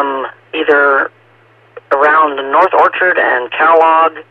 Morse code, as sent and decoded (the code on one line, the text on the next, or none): none